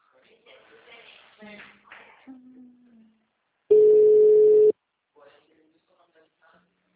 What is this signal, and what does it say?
Ringback tone (Europe, 425 Hz), with speech.